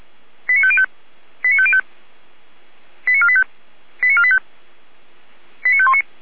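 Five-tone selective call (ZVEI), heard as quick runs of five tones, tones of about 70 ms. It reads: 89505 89505 89485 89485 89519
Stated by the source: ZVEI five-tone selective call received on an SDR.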